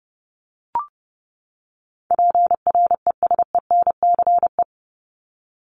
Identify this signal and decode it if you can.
Morse code, heard as a keyed tone, about 30 wpm, 700 Hz. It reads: PRESENCE
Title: Top 1751-2000 Words 30wpm